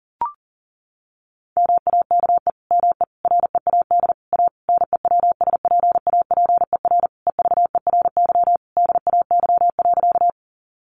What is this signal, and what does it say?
Morse code, heard as a keyed tone, about 40 wpm, 700 Hz. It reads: MAKE ME READ A NEWSPAPER EVERY DAY.
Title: Sentences from Top 1100 Words 40wpm